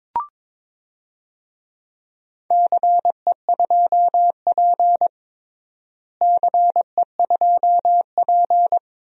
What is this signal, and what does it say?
Morse code, heard as a keyed tone, about 22 wpm, 700 Hz. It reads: CE2P CE2P
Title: DXCC-100 Call Signs - 4 Characters - Repeated 3x - 22wpm